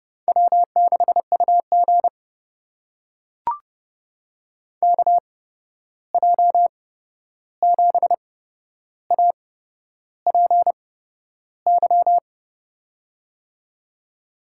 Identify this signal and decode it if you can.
Morse code, heard as a keyed tone, about 30 wpm, spaced as for 11 wpm, 700 Hz. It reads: W6UG KJ7APY
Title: USA Call Signs - 8x Character Spacing - 30wpm